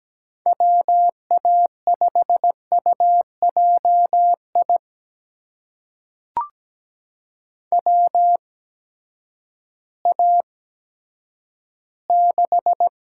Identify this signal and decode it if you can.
Morse code, heard as a keyed tone, about 17 wpm, spaced as for 10 wpm, 700 Hz. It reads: WA5UJI WA6
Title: USA Call Signs - 6 Characters - 8x Character Spacing - 17wpm